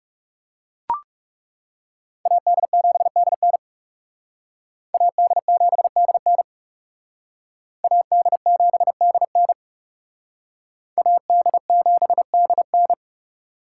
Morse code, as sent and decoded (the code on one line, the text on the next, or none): .- -.. --... -.. -. / .- -.. --... -.. -. / .- -.. --... -.. -. / .- -.. --... -.. -.
AD7DN AD7DN AD7DN AD7DN